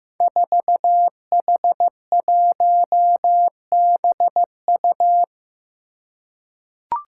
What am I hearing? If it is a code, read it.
Morse code, heard as a keyed tone, about 15 wpm, 700 Hz. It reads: 4H1BU
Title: Five Challenging Letter-Number Combinations 15wpm